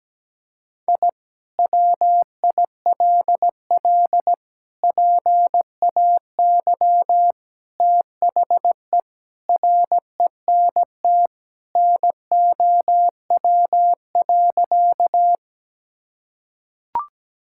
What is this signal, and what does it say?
Morse code, heard as a keyed tone, about 17 wpm, 700 Hz. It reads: I WILL PAY THE RENT NOW.
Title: Sentences from Top 1000 Words - Review Part 5 - 17wpm